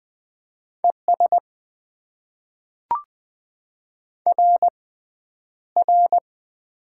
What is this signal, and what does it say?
Morse code, heard as a keyed tone, about 20 wpm, spaced as for 10 wpm, 700 Hz. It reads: ES RR
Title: Top 90 QSO Elements - 6x Character Spacing - 20wpm